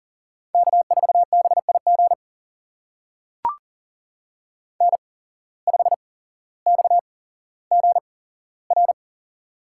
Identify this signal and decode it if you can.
Morse code, heard as a keyed tone, about 40 wpm, spaced as for 17 wpm, 700 Hz. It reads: K4BIG N5XGR
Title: USA Call Signs - 5 Characters - 8x Character Spacing - 40wpm